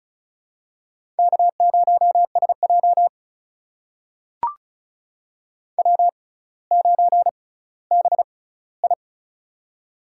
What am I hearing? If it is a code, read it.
Morse code, heard as a keyed tone, about 36 wpm, spaced as for 18 wpm, 700 Hz. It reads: K0SJ W9BI
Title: USA Call Signs - 4 Characters - 6x Character Spacing - 35wpm